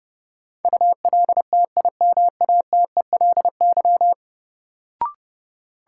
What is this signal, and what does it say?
Morse code, heard as a keyed tone, about 30 wpm, 700 Hz. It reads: ULTIMATELY
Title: Sentences from Top 1200 Words - New Word Drill Alt - 30wpm